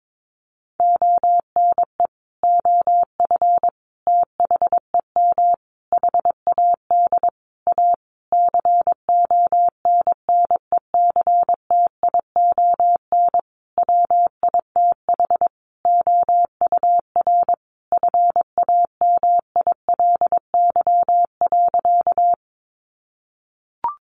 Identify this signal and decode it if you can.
Morse code, heard as a keyed tone, about 22 wpm, 700 Hz. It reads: ONE OF THEM HAD A CONNECTION WITH OUR FAMILY.
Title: Sentences from Top 1300 Words 22wpm